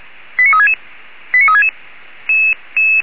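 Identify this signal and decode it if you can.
Five-tone selective call (ZVEI), heard as quick runs of five tones, tones of about 70 ms. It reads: 89366 89366
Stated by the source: ZVEI five-tone selective call 89367, received on SDR.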